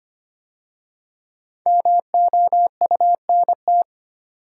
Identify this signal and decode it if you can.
Morse code, heard as a keyed tone, about 25 wpm, 700 Hz. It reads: MOUNT
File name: 5 Letter Words - First 3 Letters Given - 25wpm